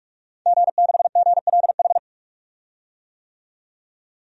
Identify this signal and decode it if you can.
Morse code, heard as a keyed tone, about 44 wpm, 700 Hz. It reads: G6GLH
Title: DXCC-100 Call Signs - 5 Characters - 45wpm